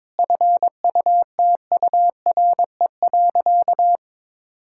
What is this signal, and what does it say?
Morse code, heard as a keyed tone, about 22 wpm, 700 Hz. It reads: FUTURE.
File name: Sentences with 7-Word Phrases - 22wpm